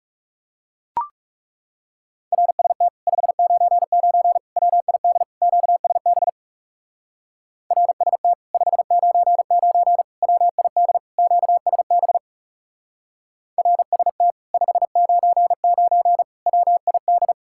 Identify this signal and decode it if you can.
Morse code, heard as a keyed tone, about 40 wpm, 700 Hz. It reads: RST 599 WID QSB RST 599 WID QSB RST 599 WID